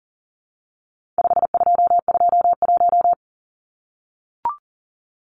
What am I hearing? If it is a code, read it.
Morse code, heard as a keyed tone, about 40 wpm, 700 Hz. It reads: 5221
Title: Numbers - 4-Digit 40wpm